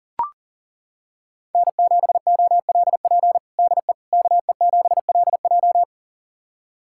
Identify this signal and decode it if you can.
Morse code, heard as a keyed tone, about 40 wpm, 700 Hz. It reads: N7OLP DE KE7LJ
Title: USA Callsign DE Callsign - 5 Characters - 40wpm